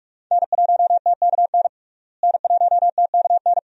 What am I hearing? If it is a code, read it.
Morse code, heard as a keyed tone, about 46 wpm, 700 Hz. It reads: N1TKN N1TKN
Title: USA Call Signs - 5 Characters - Repeated 2x - 45wpm